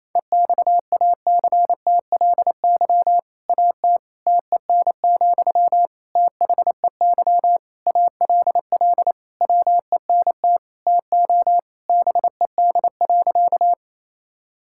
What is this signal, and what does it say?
Morse code, heard as a keyed tone, about 28 wpm, 700 Hz. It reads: EXACTLY AT TEN, THEY ALL WENT TO BED.
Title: Sentences from Top 1000 Words - Review Part 6 - 28wpm